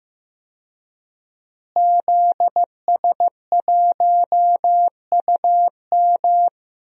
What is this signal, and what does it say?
Morse code, heard as a keyed tone, about 15 wpm, 700 Hz. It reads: ZS1UM
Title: DXCC-100 Call Signs - 5 Characters - Repeated 3x - 15wpm